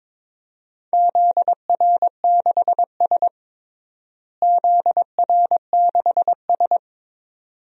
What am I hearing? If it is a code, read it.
Morse code, heard as a keyed tone, about 21 wpm, 700 Hz. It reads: ZR6S ZR6S
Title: DXCC-100 Call Signs - 4 Characters - Repeated 2x - 22wpm